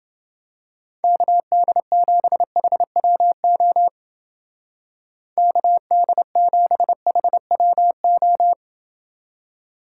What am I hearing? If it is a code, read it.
Morse code, heard as a keyed tone, about 29 wpm, 700 Hz. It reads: KD7HWO KD7HWO